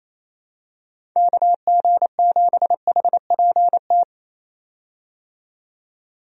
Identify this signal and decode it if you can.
Morse code, heard as a keyed tone, about 27 wpm, 700 Hz. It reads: KG7HPT
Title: USA Call Signs - 6 Characters - 28wpm